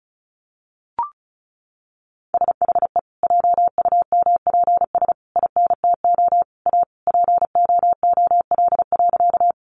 Morse code, heard as a keyed tone, about 34 wpm, 700 Hz. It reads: SHE JUMPS INTO A POOL.
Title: Additional Easy Phrases 35wpm